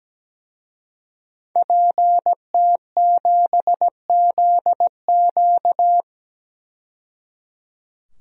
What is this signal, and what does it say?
Morse code, heard as a keyed tone, about 17 wpm, 700 Hz. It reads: PT7ZQ
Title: DXCC-100 Call Signs - 5 Characters - 17wpm